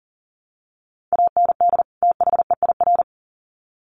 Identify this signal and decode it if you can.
Morse code, heard as a keyed tone, about 40 wpm, 700 Hz. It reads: AND THEIR
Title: Sets of 2 Words 40wpm